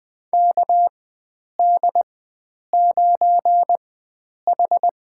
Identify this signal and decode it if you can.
Morse code, heard as a keyed tone, about 20 wpm, spaced as for 9 wpm, 700 Hz. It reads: KD9H
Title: USA Call Signs - 4 Characters - 4x Character Spacing - 20wpm